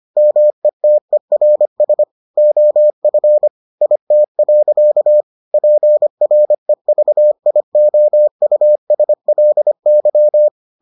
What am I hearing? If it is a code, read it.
Morse code, heard as a keyed tone, about 25 wpm, 600 Hz. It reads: METERS OF IT. PREVIOUSLY